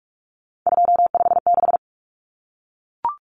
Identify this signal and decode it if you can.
Morse code, heard as a keyed tone, about 44 wpm, 700 Hz. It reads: 256
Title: Common Cut Numbers - 3-Digit 45wpm